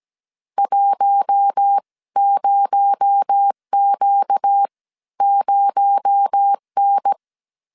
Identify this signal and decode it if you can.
Morse code, heard as a keyed tone, about 17 wpm, spaced as for 14 wpm, 800 Hz. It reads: 10Q0N
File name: CW / DET / 1 6 1500C S A 16wpm 2635